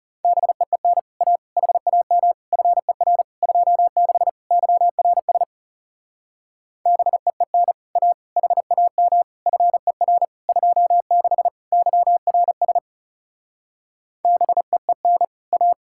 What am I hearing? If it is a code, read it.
Morse code, heard as a keyed tone, about 36 wpm, 700 Hz. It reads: BEEN A HAM FER 26 YRS BEEN A HAM FER 26 YRS BEEN A
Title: Speed-Racing Common QSO Phrases - 30wpm